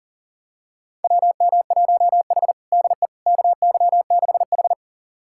Morse code, heard as a keyed tone, about 40 wpm, 700 Hz. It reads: WM1H DE KY6H
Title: USA Callsign DE Callsign - 4 Characters - 40wpm